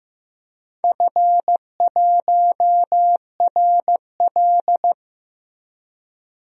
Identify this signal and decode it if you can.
Morse code, heard as a keyed tone, about 15 wpm, 700 Hz. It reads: F1RL
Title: DXCC-100 Call Signs - 4 Characters - Repeated 2x - 15wpm